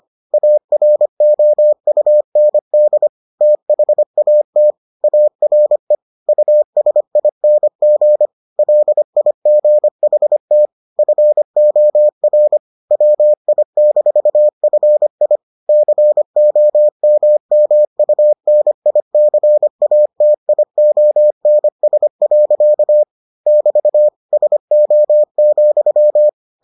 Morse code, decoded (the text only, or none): AROUND THAT ARE USING LIGHT FOR WI-FI COMMUNICATIONS. <BT> SO,